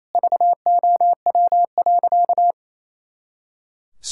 Morse code, heard as a keyed tone, about 28 wpm, 700 Hz. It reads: VOW.
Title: Pangram Challenge v2 - 28wpm